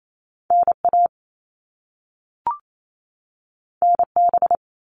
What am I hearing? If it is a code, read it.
Morse code, heard as a keyed tone, about 28 wpm, 700 Hz. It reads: NA NB
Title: All Cut Numbers - 2-Digit 28wpm